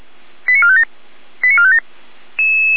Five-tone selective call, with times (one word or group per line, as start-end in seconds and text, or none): 0.5-0.8 89457
1.4-1.8 89457